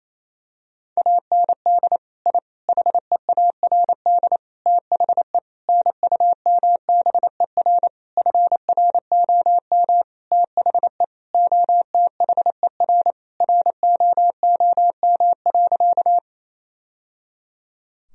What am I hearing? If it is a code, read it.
Morse code, heard as a keyed tone, about 28 wpm, 700 Hz. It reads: AND I HEARD THE NUMBER FROM THE OTHER ROOM.
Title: Sentences from Top 1000 Words - Review Part 1 - 28wpm